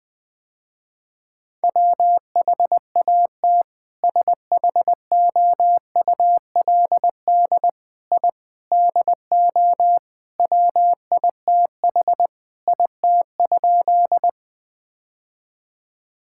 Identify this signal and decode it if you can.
Morse code, heard as a keyed tone, about 20 wpm, 700 Hz. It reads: WHAT SHOULD I DO WITH IT?
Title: Sentences with 4-Word Phrases - Spoken Beginning - 20wpm